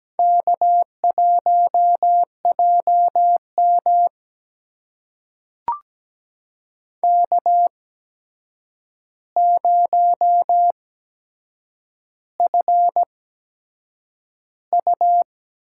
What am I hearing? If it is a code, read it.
Morse code, heard as a keyed tone, about 17 wpm, spaced as for 7 wpm, 700 Hz. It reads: K1JM K0FU
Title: USA Call Signs - 4 Characters - 8x Character Spacing - 17wpm